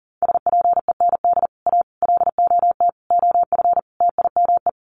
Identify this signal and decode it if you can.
Morse code, heard as a keyed tone, about 40 wpm, 700 Hz. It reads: SPEND A LOT OF TIME
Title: Sets of 5 Words 40wpm